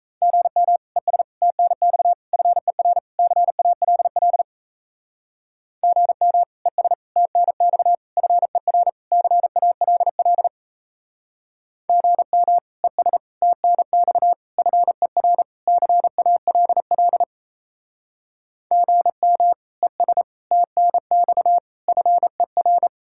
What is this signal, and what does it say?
Morse code, heard as a keyed tone, about 34 wpm, 700 Hz. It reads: GM ES TNX FER CALL GM ES TNX FER CALL GM ES TNX FER CALL GM ES TNX FER
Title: Speed-Racing Common QSO Phrases Encore - 28wpm